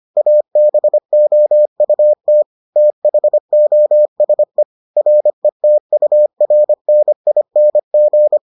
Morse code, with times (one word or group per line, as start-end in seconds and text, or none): 0.2-2.4 ABOUT
2.8-4.6 THOSE
5.0-8.4 RETURNING